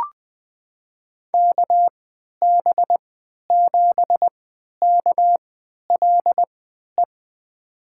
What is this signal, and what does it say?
Morse code, heard as a keyed tone, about 20 wpm, spaced as for 11 wpm, 700 Hz. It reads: KB7KLE
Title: USA Call Signs - 6 Characters - 3x Character Spacing - 20wpm